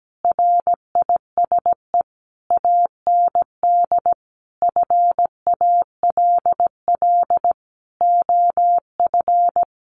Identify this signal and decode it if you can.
Morse code, heard as a keyed tone, about 17 wpm, 700 Hz. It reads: RISE AND FALL OF
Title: Sentences from Top 1500 Words 17wpm